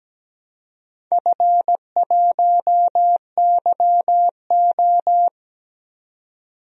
Morse code, heard as a keyed tone, about 17 wpm, 700 Hz. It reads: F1YO
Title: DXCC-100 Call Signs - 4 Characters - 17wpm